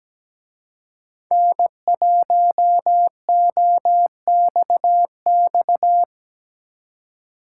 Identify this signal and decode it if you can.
Morse code, heard as a keyed tone, about 20 wpm, 700 Hz. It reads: N1OXX
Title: Speed Racing - USA 1x3 Call Signs 17wpm